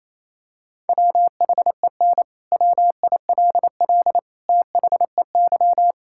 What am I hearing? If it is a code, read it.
Morse code, heard as a keyed tone, about 27 wpm, 700 Hz. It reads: WHEN WILL THEY
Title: Sentences from Top 1000 Words - Review Part 5 - 28wpm